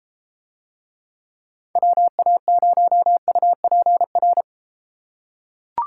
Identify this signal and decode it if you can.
Morse code, heard as a keyed tone, about 31 wpm, 700 Hz. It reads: WA0UPR